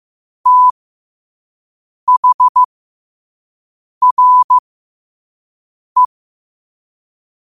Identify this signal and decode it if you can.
Morse code, heard as a keyed tone, about 15 wpm, spaced as for 5 wpm, 1000 Hz. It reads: THRE
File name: qotdfiles / 5wpm